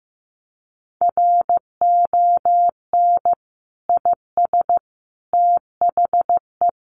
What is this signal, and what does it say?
Morse code, heard as a keyed tone, about 15 wpm, 700 Hz. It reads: RON IS THE
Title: Additional Easy Phrases 15wpm